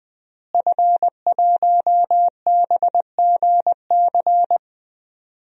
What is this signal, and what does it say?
Morse code, heard as a keyed tone, about 21 wpm, 700 Hz. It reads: F1BGC
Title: DXCC-100 Call Signs - 5 Characters - Repeated 3x - 20wpm